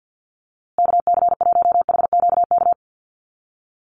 Morse code, heard as a keyed tone, about 48 wpm, 700 Hz. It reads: KC1HQK
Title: USA Call Signs Repeated 3x 50wpm